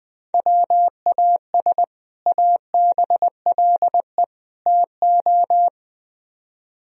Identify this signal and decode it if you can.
Morse code, heard as a keyed tone, about 20 wpm, 700 Hz. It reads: WAS ABLE TO
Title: Speed-Racing - Sets of 3 Words 17wpm